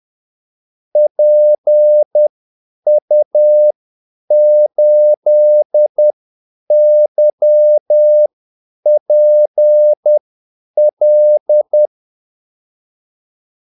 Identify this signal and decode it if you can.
Morse code, heard as a keyed tone, about 10 wpm, spaced as for 8 wpm, 600 Hz. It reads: PU8YPL
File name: Callsigns-10-8wpm